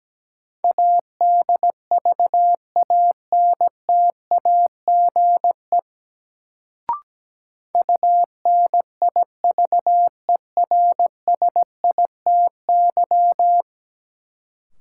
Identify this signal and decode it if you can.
Morse code, heard as a keyed tone, about 17 wpm, 700 Hz. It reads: ADVANTAGE UNIVERSITY
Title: Sets of 4 Words - New Word Drill - 17wpm